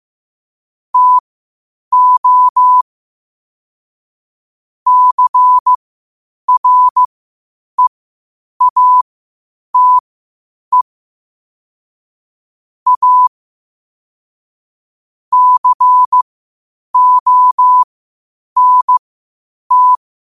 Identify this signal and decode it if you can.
Morse code, heard as a keyed tone, about 15 wpm, spaced as for 8 wpm, 1000 Hz. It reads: TO CREATE A CONT
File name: qotdfiles / 8wpm